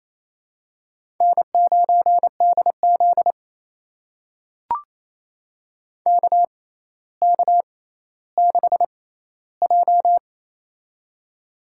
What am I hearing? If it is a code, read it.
Morse code, heard as a keyed tone, about 27 wpm, spaced as for 14 wpm, 700 Hz. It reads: N9DZ KK6J
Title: USA Call Signs - 6x Character Spacing - 28wpm